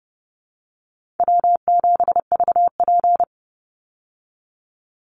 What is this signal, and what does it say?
Morse code, heard as a keyed tone, about 30 wpm, 700 Hz. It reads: W7VP
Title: Call Signs Repeated 2x - 30wpm